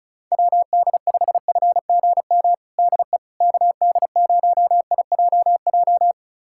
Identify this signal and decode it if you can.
Morse code, heard as a keyed tone, about 34 wpm, 700 Hz. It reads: WD5FGM DE KD0IJJ
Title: USA Callsign DE Callsign - 6 Characters - 35wpm